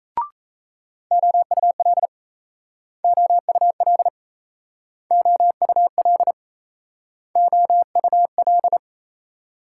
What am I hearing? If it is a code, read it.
Morse code, heard as a keyed tone, about 37 wpm, 700 Hz. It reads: OUL OUL OUL OUL